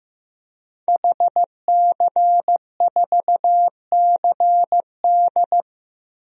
Morse code, heard as a keyed tone, about 15 wpm, 700 Hz. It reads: HC4CD
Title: DXCC-100 Call Signs - 5 Characters - 15wpm